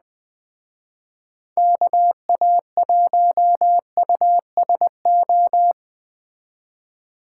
Morse code, heard as a keyed tone, about 19 wpm, 700 Hz. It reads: KA1USO